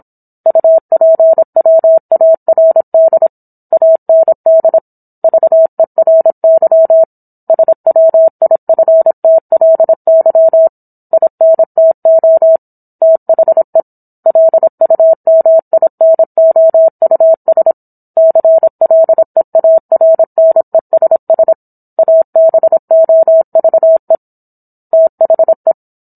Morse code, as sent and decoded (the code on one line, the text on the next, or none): ..- .--. .-- .- .-. -.. / .- -. -.. / ...- . .-. -.-- / ... .-- .. ..-. - .-.. -.-- / .. -. - --- / - .... / .-.. ..- -- .. -. --- ..- ... / -.-. .-.. .- .-. -. ... ... / .- -... --- ...- . / - .... .
UPWARD AND VERY SWIFTLY INTO THE LUMINOUS CLEARNESS ABOVE THE